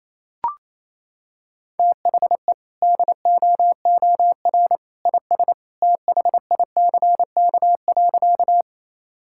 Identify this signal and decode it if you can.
Morse code, heard as a keyed tone, about 28 wpm, 700 Hz. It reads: THE DOOR IS THICK.
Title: Sentences from Top 1600 Words 28wpm